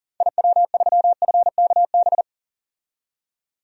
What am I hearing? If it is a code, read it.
Morse code, heard as a keyed tone, about 40 wpm, 700 Hz. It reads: IW3FKB
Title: DXCC-100 Call Signs - 6 Characters - 40wpm